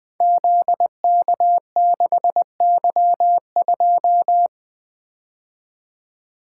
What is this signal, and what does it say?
Morse code, heard as a keyed tone, about 20 wpm, 700 Hz. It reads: ZK6Y2
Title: Code Groups - 5 Letter:Numbers - 20wpm